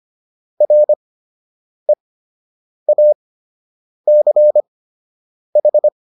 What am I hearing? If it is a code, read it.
Morse code, heard as a keyed tone, about 25 wpm, spaced as for 8 wpm, 600 Hz. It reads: REACH